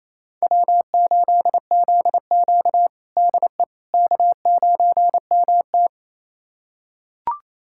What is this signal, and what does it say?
Morse code, heard as a keyed tone, about 28 wpm, 700 Hz. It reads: W8ZQ DE K9MT
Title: USA Callsign DE Callsign - 4 Characters - 28wpm